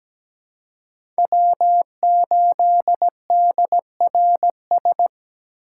Morse code, heard as a keyed tone, about 17 wpm, 700 Hz. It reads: W8DRS
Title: USA Call Signs - 5 Characters - 3x Character Spacing - 17wpm